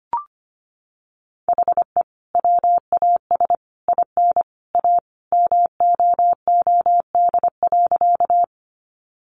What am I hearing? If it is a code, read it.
Morse code, heard as a keyed tone, about 25 wpm, 700 Hz. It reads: HE WAS IN A MOOD.
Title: Sentences from Top 1100 Words 25wpm